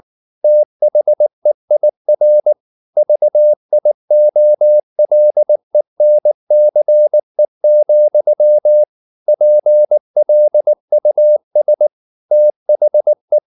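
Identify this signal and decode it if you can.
Morse code, heard as a keyed tone, about 19 wpm, 600 Hz. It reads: THEIR VIOLENCE, PLUS THE
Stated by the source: Morse code at 19 wpm from CW med Gnister, reading Ken Follett's novel Never in English.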